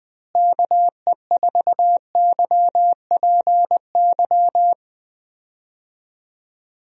Morse code, heard as a keyed tone, about 20 wpm, 700 Hz. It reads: KE4YPY
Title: USA Call Signs - 6 Characters - Repeated 3x - 20wpm